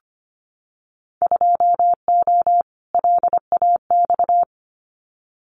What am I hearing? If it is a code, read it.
Morse code, heard as a keyed tone, about 25 wpm, 700 Hz. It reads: 2O LAX
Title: Winter Field Day Exchanges 25wpm